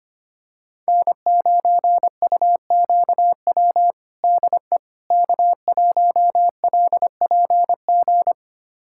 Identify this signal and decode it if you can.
Morse code, heard as a keyed tone, about 25 wpm, 700 Hz. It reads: N9UQW DE K1LPG
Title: USA Callsign DE Callsign - 5 Characters - 25wpm